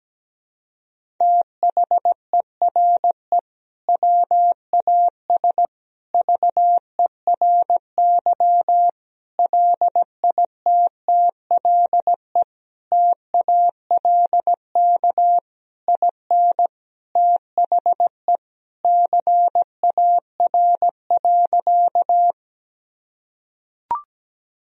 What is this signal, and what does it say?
Morse code, heard as a keyed tone, about 17 wpm, 700 Hz. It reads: THERE WAS VERY LITTLE TALK IN THE CAR.
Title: Sentences from Top 1000 Words - Review Part 1 - 17wpm